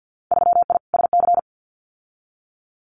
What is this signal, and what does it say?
Morse code, heard as a keyed tone, about 50 wpm, 700 Hz. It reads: 3I SC
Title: Winter Field Day Exchanges Encore 50wpm